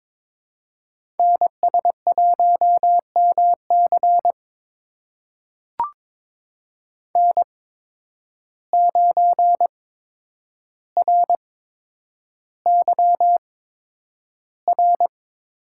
Morse code, heard as a keyed tone, about 22 wpm, spaced as for 9 wpm, 700 Hz. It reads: NS1MC N9RYR
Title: USA Call Signs - 8x Character Spacing - 22wpm